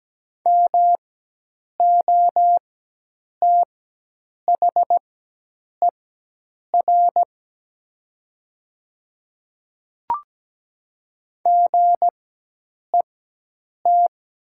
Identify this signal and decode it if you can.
Morse code, heard as a keyed tone, about 17 wpm, spaced as for 8 wpm, 700 Hz. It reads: MOTHER GET
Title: 4x Character Spacing - Top 200 Words - 17wpm